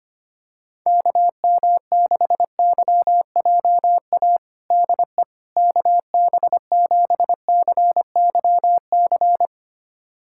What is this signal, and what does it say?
Morse code, heard as a keyed tone, about 25 wpm, 700 Hz. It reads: KM6YJA DE KB7CYC
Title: USA Callsign DE Callsign - 6 Characters - 25wpm